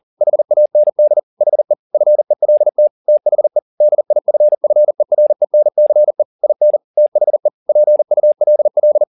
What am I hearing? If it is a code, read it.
Morse code, heard as a keyed tone, about 40 wpm, 600 Hz. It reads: HAND HE FELT THE DIFFERENCE IN THE PULL